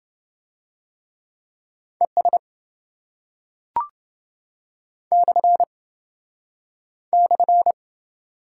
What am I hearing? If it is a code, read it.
Morse code, heard as a keyed tone, about 27 wpm, 700 Hz. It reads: ES / /